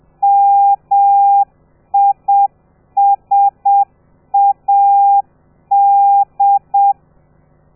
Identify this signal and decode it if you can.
Morse code, heard as a keyed tone, about 7 wpm, 800 Hz. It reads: MISAD
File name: telegrafia / i2tfj / Lez13 30